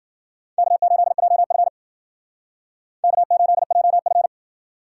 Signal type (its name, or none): Morse code